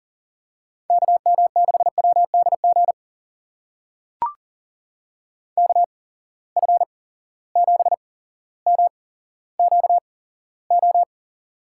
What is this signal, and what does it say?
Morse code, heard as a keyed tone, about 40 wpm, spaced as for 17 wpm, 700 Hz. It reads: KM6WDG KF7MQO
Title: USA Call Signs - 6 Characters - 8x Character Spacing - 40wpm